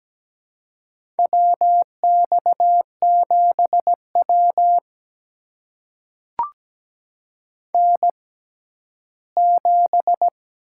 Morse code, heard as a keyed tone, about 17 wpm, spaced as for 12 wpm, 700 Hz. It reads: WX7W N7
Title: USA Call Signs - 4 Characters - 6x Character Spacing - 17wpm